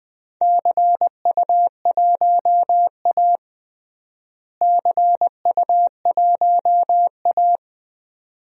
Morse code, heard as a keyed tone, about 21 wpm, 700 Hz. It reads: CU1A CU1A
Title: DXCC-100 Call Signs - 4 Characters - Repeated 2x - 20wpm